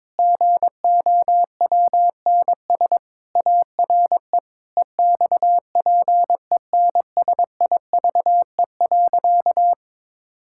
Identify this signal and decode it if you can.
Morse code, heard as a keyed tone, about 22 wpm, 700 Hz. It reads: GOWNS ARE EXPENSIVE.
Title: Pangram Challenge v2 - 22wpm